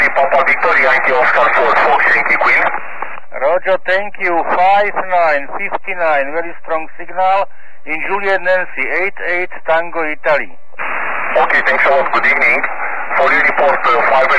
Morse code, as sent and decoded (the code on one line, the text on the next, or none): none